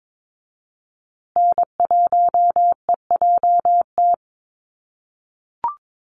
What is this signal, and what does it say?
Morse code, heard as a keyed tone, about 22 wpm, 700 Hz. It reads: N1EJT
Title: USA Call Signs Repeated 2x 22wpm